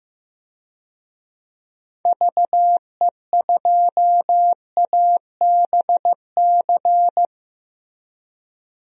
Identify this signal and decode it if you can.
Morse code, heard as a keyed tone, about 15 wpm, 700 Hz. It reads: VE2ABC
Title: DXCC-100 Call Signs - 6 Characters - Repeated 3x - 15wpm